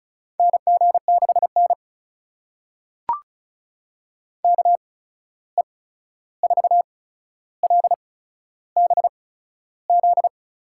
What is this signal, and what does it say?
Morse code, heard as a keyed tone, about 36 wpm, spaced as for 13 wpm, 700 Hz. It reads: NG6N KE4LBZ
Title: USA Call Signs - 8x Character Spacing - 35wpm